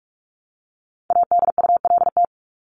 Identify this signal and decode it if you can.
Morse code, heard as a keyed tone, about 44 wpm, 700 Hz. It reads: ADULT
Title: Top 1501-1750 Words 45wpm